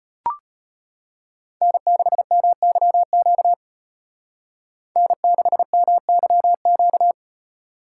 Morse code, heard as a keyed tone, about 32 wpm, 700 Hz. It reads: N6MYQ N6MYQ